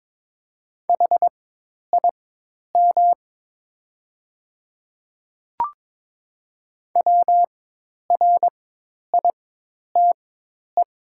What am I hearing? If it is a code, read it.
Morse code, heard as a keyed tone, about 22 wpm, spaced as for 10 wpm, 700 Hz. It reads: HIM WRITE
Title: 4x Character Spacing - Top 100 Words - 22wpm